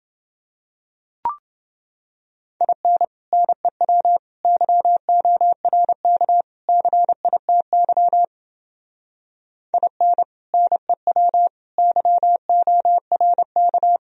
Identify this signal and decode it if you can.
Morse code, heard as a keyed tone, about 30 wpm, 700 Hz. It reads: IN NEW YORK CITY IN NEW YORK